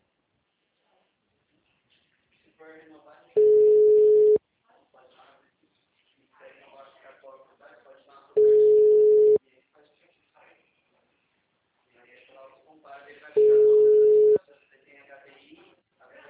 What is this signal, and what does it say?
Ringback tone (Europe, 425 Hz), with speech.
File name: out- -2045- - - .18718